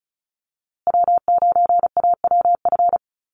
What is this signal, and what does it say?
Morse code, heard as a keyed tone, about 34 wpm, 700 Hz. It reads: W9AWF
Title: USA Call Signs Repeated 2x 35wpm